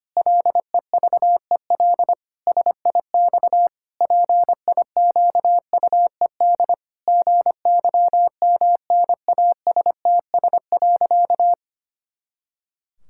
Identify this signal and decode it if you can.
Morse code, heard as a keyed tone, about 25 wpm, 700 Hz. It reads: LEVEL SIX PIQUED GYMNASTS.
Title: Pangram Challenge v2 - 25wpm